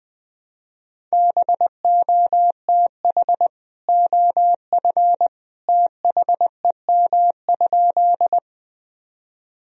Morse code, heard as a keyed tone, about 20 wpm, 700 Hz. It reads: BOTH OF THEM?
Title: Sentences from Top 1000 Words - Review Part 1 - 20wpm